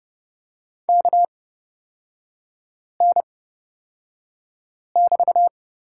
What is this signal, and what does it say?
Morse code, heard as a keyed tone, about 30 wpm, spaced as for 5 wpm, 700 Hz. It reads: KN<BT>
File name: Lesson 36e - BT - Three Characters 30wpm